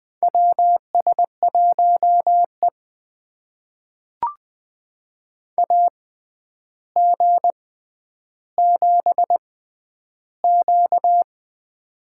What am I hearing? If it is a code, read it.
Morse code, heard as a keyed tone, about 20 wpm, spaced as for 10 wpm, 700 Hz. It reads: WS1E AG7Q
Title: USA Call Signs - 4 Characters - 6x Character Spacing - 20wpm